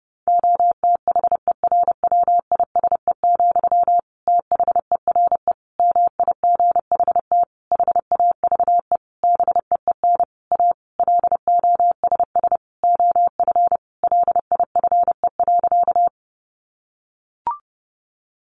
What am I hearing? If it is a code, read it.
Morse code, heard as a keyed tone, about 30 wpm, 700 Hz. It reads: OTHERWISE, THERE MIGHT HAVE BEEN A LOSS OF LIFE.
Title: Sentences from Top 1400 Words 30wpm